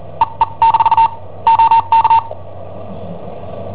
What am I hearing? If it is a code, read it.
Morse code, heard as a keyed tone, about 43 wpm, spaced as for 32 wpm, 910 Hz. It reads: EE- OK
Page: Cassetta esercizi preparata da I7OHP per IW5ELP